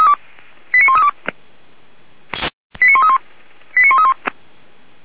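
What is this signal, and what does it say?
Five-tone selective call (ZVEI), heard as quick runs of five tones, tones of about 70 ms. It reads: 89132 89131 89131
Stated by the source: ZVEI five-tone selective call received by SDR, logged by Feuerwehr Orlamünde.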